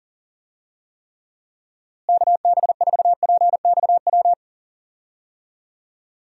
Morse code, heard as a keyed tone, about 40 wpm, 700 Hz. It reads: KB4PXW
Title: USA Call Signs - 6 Characters - Repeated 2x - 40wpm